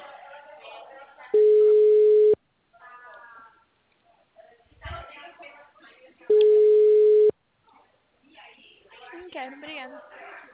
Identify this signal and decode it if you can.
Ringback tone (Europe, 425 Hz), with speech.
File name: out- -2012- - - .341